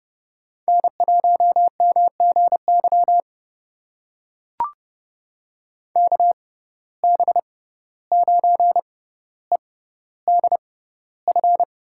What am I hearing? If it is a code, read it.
Morse code, heard as a keyed tone, about 32 wpm, spaced as for 15 wpm, 700 Hz. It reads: N1MGY KB9EDF
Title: USA Call Signs - 6x Character Spacing - 30wpm